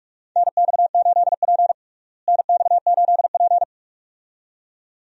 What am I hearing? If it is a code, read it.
Morse code, heard as a keyed tone, about 41 wpm, 700 Hz. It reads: NX8P NX8P
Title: USA Call Signs - 4 Characters - Repeated 2x - 45wpm